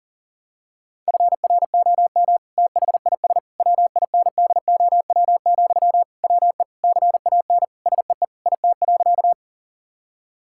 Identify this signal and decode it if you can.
Morse code, heard as a keyed tone, about 37 wpm, 700 Hz. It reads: FROM THIS WINDOW, WE CAN SEE IT.
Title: Sentences from Top 1600 Words 40wpm